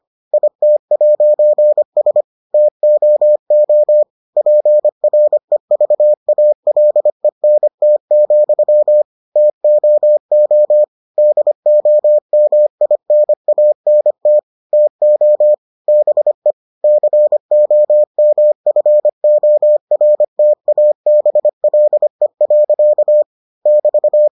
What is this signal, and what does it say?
Morse code, heard as a keyed tone, about 25 wpm, 600 Hz. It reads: IT'S TOO PREVALENT, TOO DOMINANT TO BE COMFORTABLE. <BT>